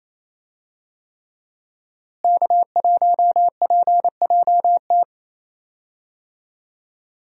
Morse code, heard as a keyed tone, about 28 wpm, 700 Hz. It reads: K1PJT
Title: USA Call Signs - 5 Characters - Repeated 3x - 28wpm